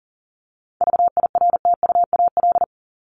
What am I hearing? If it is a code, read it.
Morse code, heard as a keyed tone, about 40 wpm, 700 Hz. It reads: VIRTUAL
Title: LY Suffix - Top 100 Words with Base Words 40wpm